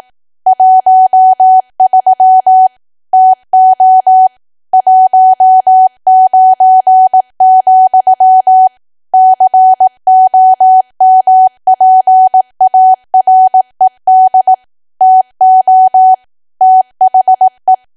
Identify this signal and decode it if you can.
Morse code, heard as a keyed tone, about 18 wpm, 750 Hz. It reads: 13 TO 19, COMPARED TO THE
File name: cwa / W1AW-Level2 / 18WPM